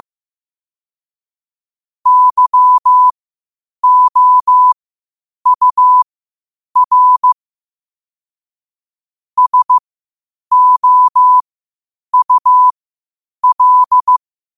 Morse code, heard as a keyed tone, about 15 wpm, spaced as for 8 wpm, 1000 Hz. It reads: YOUR SOUL
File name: qotdfiles / 8wpm